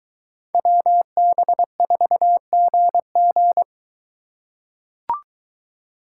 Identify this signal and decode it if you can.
Morse code, heard as a keyed tone, about 24 wpm, 700 Hz. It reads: WB4GG